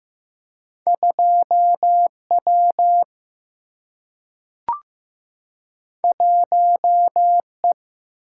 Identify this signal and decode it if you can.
Morse code, heard as a keyed tone, about 15 wpm, 700 Hz. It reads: 2W 1E
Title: Single-Letter Single-Number 15wpm